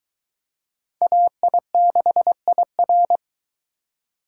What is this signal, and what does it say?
Morse code, heard as a keyed tone, about 22 wpm, 700 Hz. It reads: AI6IR